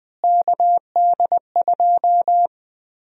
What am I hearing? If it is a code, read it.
Morse code, heard as a keyed tone, about 20 wpm, 700 Hz. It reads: KD2
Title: USA Call Sign Prefixes - Repeated 2x - 20wpm